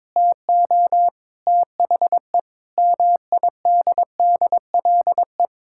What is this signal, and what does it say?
Morse code, heard as a keyed tone, about 22 wpm, 700 Hz. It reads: TO THE MIDDLE
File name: practice / 22wpm / Sentences with 3-Word Phrases - 22wpm